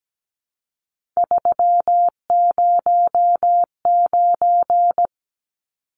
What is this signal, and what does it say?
Morse code, heard as a keyed tone, about 17 wpm, 700 Hz. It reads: 309
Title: Numbers - 3-Digit 17wpm